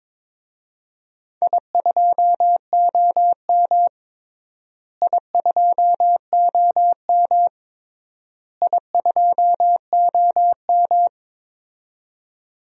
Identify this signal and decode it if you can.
Morse code, heard as a keyed tone, about 22 wpm, 700 Hz. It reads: I2OM I2OM I2OM
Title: DXCC-100 Call Signs - 4 Characters - Repeated 3x - 22wpm